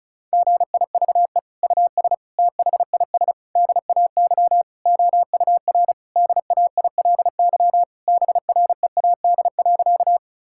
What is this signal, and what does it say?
Morse code, heard as a keyed tone, about 36 wpm, 700 Hz. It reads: GIVE US THIS DAY OUR DAILY BREAD.
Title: Sentences from Top 1600 Words 35wpm